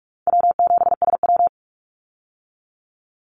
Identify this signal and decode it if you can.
Morse code, heard as a keyed tone, about 46 wpm, 700 Hz. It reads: W7SW
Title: Easy Call Signs - 45wpm